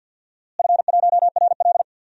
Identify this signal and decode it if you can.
Morse code, heard as a keyed tone, about 52 wpm, 700 Hz. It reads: F1RL
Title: DXCC-100 Call Signs - 4 Characters - Repeated 2x - 50wpm